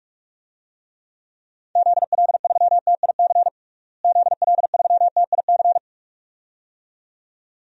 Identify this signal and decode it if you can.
Morse code, heard as a keyed tone, about 46 wpm, 700 Hz. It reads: ZL3TIC ZL3TIC
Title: DXCC-100 Call Signs - 6 Characters - Repeated 2x - 45wpm